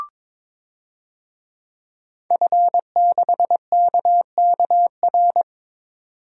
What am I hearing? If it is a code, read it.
Morse code, heard as a keyed tone, about 22 wpm, 700 Hz. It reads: F6KKR
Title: DXCC-100 Call Signs - 5 Characters - 22wpm